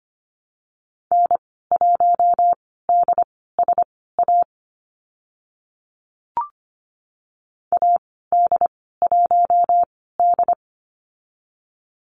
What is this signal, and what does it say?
Morse code, heard as a keyed tone, about 25 wpm, spaced as for 16 wpm, 690 Hz. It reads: N1DSA AD1D
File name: Lesson 15g - Number 5 - Call Signs 25wpm